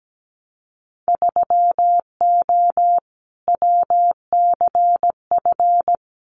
Morse code, heard as a keyed tone, about 17 wpm, 700 Hz. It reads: 3O WCF
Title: Winter Field Day Exchanges 17wpm